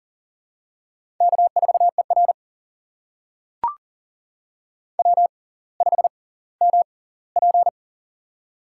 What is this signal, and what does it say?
Morse code, heard as a keyed tone, about 38 wpm, spaced as for 20 wpm, 700 Hz. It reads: K4ER W5MP
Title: USA Call Signs - 4 Characters - 6x Character Spacing - 40wpm